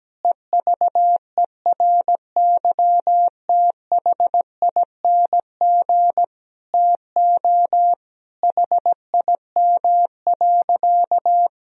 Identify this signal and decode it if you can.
Morse code, heard as a keyed tone, about 17 wpm, 700 Hz. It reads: EVERYTHING TO HIM.
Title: Sentences from Top 1000 Words - Review Part 3 - 17wpm